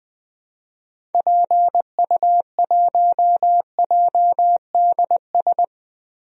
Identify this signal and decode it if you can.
Morse code, heard as a keyed tone, about 20 wpm, 700 Hz. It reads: PU1JDS
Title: DXCC-100 Call Signs - 6 Characters - Repeated 2x - 20wpm